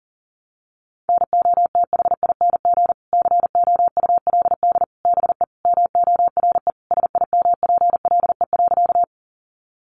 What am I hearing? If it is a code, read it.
Morse code, heard as a keyed tone, about 40 wpm, 700 Hz. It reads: NOTHING COULD BE MORE SIMPLE.
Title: Sentences from Top 500 Words 40wpm